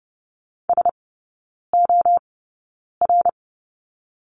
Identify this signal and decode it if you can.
Morse code, heard as a keyed tone, about 30 wpm, spaced as for 9 wpm, 700 Hz. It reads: SOR
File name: Lesson 18e - Letter m - Three Characters 30wpm